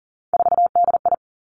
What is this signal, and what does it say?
Morse code, heard as a keyed tone, about 42 wpm, 700 Hz. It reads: 4DI
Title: Three Challenging Letter-Number Combinations 40wpm